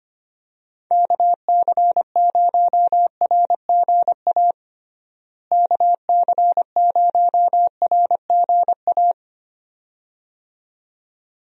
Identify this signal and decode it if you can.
Morse code, heard as a keyed tone, about 25 wpm, 700 Hz. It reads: KC0RGA KC0RGA
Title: USA Call Signs - 6 Characters - Repeated 2x - 25wpm